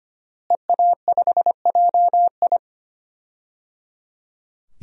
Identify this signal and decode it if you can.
Morse code, heard as a keyed tone, about 25 wpm, 700 Hz. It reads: EA5JI
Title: DXCC-100 Call Signs - 5 Characters - Repeated 3x - 25wpm